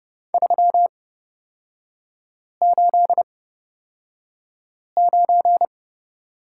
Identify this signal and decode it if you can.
Morse code, heard as a keyed tone, about 29 wpm, spaced as for 5 wpm, 700 Hz. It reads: 389
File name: Lesson 38e - Letter q - Three Characters 30wpm